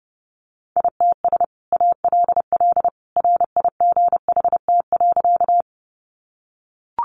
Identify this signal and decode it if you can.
Morse code, heard as a keyed tone, about 30 wpm, 700 Hz. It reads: ITS ALL RIGHT.
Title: Sentences from Top 600 Words 30wpm